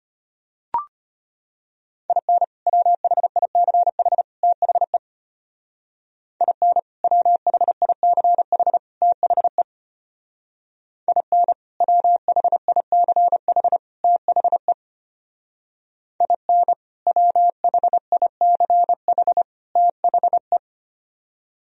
Morse code, heard as a keyed tone, about 31 wpm, 700 Hz. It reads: IN WHICH THE IN WHICH THE IN WHICH THE IN WHICH THE
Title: Speed-Racing - Sets of 3 Words 25wpm